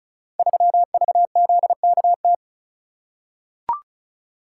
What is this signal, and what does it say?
Morse code, heard as a keyed tone, about 36 wpm, 700 Hz. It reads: 3VZKT
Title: Code Groups - 5 Letter:Numbers - 35wpm